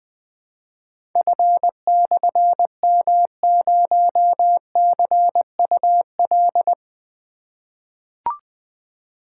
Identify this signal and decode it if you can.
Morse code, heard as a keyed tone, about 20 wpm, 700 Hz. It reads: F/M0CUL
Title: Difficult Call Signs 20wpm